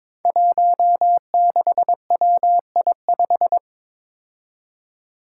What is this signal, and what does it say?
Morse code, heard as a keyed tone, about 22 wpm, 700 Hz. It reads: 16WI5
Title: Five Challenging Letter-Number Combinations 22wpm